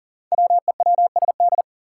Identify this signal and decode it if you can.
Morse code, heard as a keyed tone, about 40 wpm, 700 Hz. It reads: WEWSD
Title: Five Challenging Letter-Number Combinations 40wpm